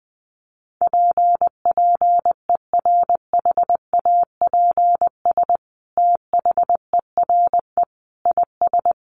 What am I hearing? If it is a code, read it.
Morse code, heard as a keyed tone, about 20 wpm, 700 Hz. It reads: PPERHAPS THERE IS
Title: Sentences from Top 1100 Words 20wpm